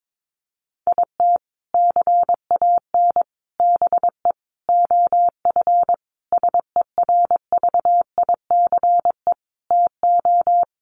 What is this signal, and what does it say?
Morse code, heard as a keyed tone, about 22 wpm, 700 Hz. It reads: IT CAN BE OF SERVICE TO
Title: Sentences from Top 1200 Words 22wpm